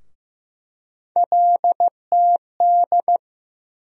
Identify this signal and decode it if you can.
Morse code, heard as a keyed tone, about 15 wpm, 700 Hz. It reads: LTD
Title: USA Call Sign Postfixes - 3 Letters 15wpm